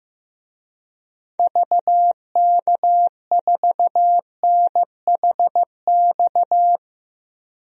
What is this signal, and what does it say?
Morse code, heard as a keyed tone, about 15 wpm, 700 Hz. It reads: VK4NHX
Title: DXCC-100 Call Signs - 6 Characters - 15wpm